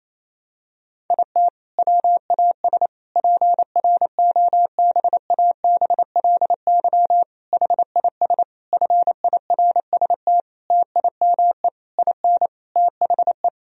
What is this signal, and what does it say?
Morse code, heard as a keyed tone, about 28 wpm, 700 Hz. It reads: IT WAS PROBABLY HIS FIRST TIME IN THE
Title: Sentences from Top 1000 Words - Review Part 4 - 28wpm